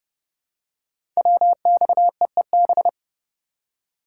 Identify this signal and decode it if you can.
Morse code, heard as a keyed tone, about 29 wpm, 700 Hz. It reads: WXEEB
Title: Code Groups - 5 Letters - 30wpm